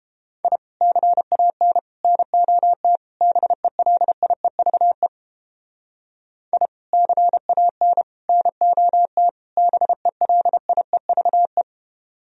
Morse code, heard as a keyed tone, about 32 wpm, 700 Hz. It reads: I CAN NOT BELIEVE I CAN NOT BELIEVE